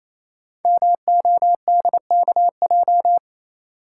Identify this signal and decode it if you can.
Morse code, heard as a keyed tone, about 27 wpm, 700 Hz. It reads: MODKJ
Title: Code Groups - 5 Letter:Numbers - 28wpm